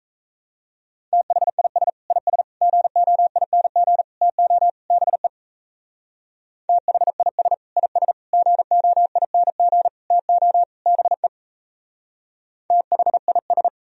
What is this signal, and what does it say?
Morse code, heard as a keyed tone, about 35 wpm, 700 Hz. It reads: THIS IS GOING TO BE THIS IS GOING TO BE THIS
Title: Speed-Racing - Sets of 5 Words 28wpm